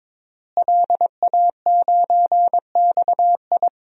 Morse code, heard as a keyed tone, about 22 wpm, 700 Hz. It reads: LA9XI
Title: DXCC-100 Call Signs - 5 Characters - Repeated 3x - 22wpm